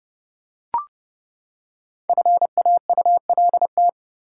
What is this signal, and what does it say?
Morse code, heard as a keyed tone, about 31 wpm, 700 Hz. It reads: FAULT
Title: Five Letter Words - 30wpm